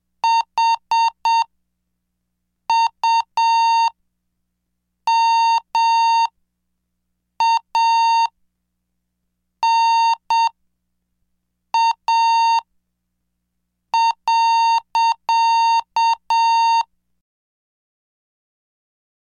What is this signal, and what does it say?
Morse code, heard as a keyed tone, about 7 wpm, spaced as for 5 wpm, 920 Hz. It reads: HUMANA.
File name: cursos / CDestudo / 26 Track 26